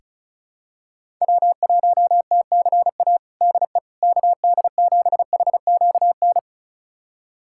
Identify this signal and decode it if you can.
Morse code, heard as a keyed tone, about 34 wpm, 700 Hz. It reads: W1TCA DE KD7HQN
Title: USA Callsign DE Callsign 35wpm